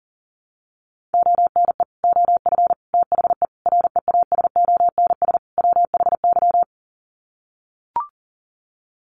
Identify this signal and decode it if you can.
Morse code, heard as a keyed tone, about 40 wpm, 700 Hz. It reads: ONE OF THE REASONS WHY
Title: Sets of 5 Words 40wpm